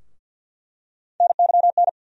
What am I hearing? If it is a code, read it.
Morse code, heard as a keyed tone, about 47 wpm, 700 Hz. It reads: NYN
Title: USA Call Sign Postfixes - 3 Letters 50wpm